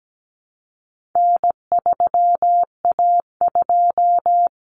Morse code, heard as a keyed tone, about 17 wpm, 700 Hz. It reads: N3A2
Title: Common Cut Numbers - 4-Digit 17wpm